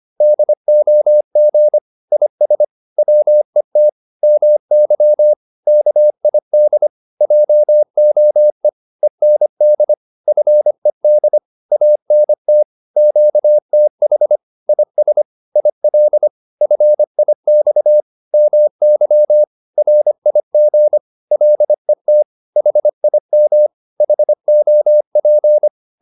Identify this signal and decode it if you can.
Morse code, heard as a keyed tone, about 25 wpm, 600 Hz. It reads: DOG IS WET MY KID JOE END FED ANT QTH IS IL FIX MY RIG LET HIM HOP